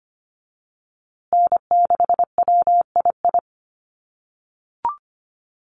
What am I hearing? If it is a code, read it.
Morse code, heard as a keyed tone, about 25 wpm, 700 Hz. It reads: N6WII
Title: USA Call Signs Repeated 3x 25wpm